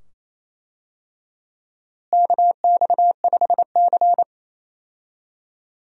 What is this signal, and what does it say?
Morse code, heard as a keyed tone, about 27 wpm, 700 Hz. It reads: KX5C
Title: USA Call Signs - 4 Characters - Repeated 2x - 28wpm